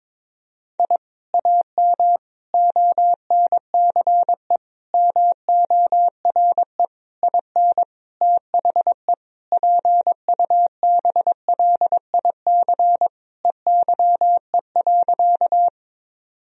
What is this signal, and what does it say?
Morse code, heard as a keyed tone, about 21 wpm, 700 Hz. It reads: I AM ONCE MORE IN THE PUBLIC EYE.
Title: Sentences from Top 1000 Words - Review Part 3 - 22wpm